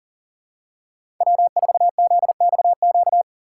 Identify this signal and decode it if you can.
Morse code, heard as a keyed tone, about 40 wpm, 700 Hz. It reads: W4ZXQ
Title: USA Call Signs - 5 Characters - 4x Character Spacing - 40wpm